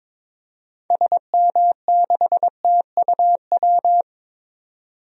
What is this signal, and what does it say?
Morse code, heard as a keyed tone, about 21 wpm, 700 Hz. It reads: SM6TUW
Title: DXCC-100 Call Signs - 6 Characters - 22wpm